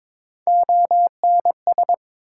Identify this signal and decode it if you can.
Morse code, heard as a keyed tone, about 21 wpm, 700 Hz. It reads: ONS
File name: ARRL Sections v2 - 22wpm